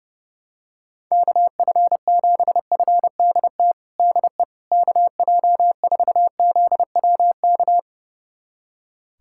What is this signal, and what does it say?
Morse code, heard as a keyed tone, about 32 wpm, 700 Hz. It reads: KF7FDT DE KJ4ZWK
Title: USA Callsign DE Callsign - 6 Characters - 30wpm